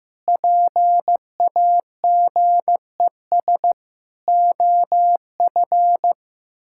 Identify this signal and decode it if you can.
Morse code, heard as a keyed tone, about 15 wpm, 700 Hz. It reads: PAGES OF
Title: Pangram Challenge v2 - 15wpm